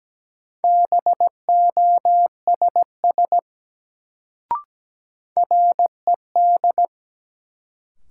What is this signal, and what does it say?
Morse code, heard as a keyed tone, about 17 wpm, 700 Hz. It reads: BOSS RED
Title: Sentences from Top 700 Words - New Word Drill - 17wpm